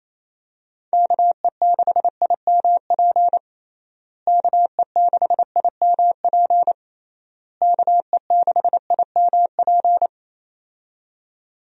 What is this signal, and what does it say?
Morse code, heard as a keyed tone, about 28 wpm, 700 Hz. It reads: KE6IMP KE6IMP KE6IMP
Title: USA Call Signs - 6 Characters - Repeated 3x - 28wpm